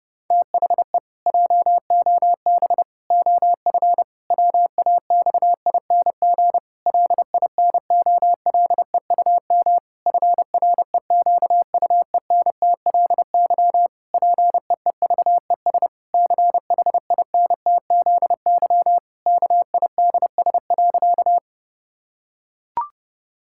Morse code, heard as a keyed tone, about 30 wpm, 700 Hz. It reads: THE JOB OF WAXING LINOLEUM FREQUENTLY PEEVES CHINTZY KIDS.
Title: Pangram Challenge v2 - 30wpm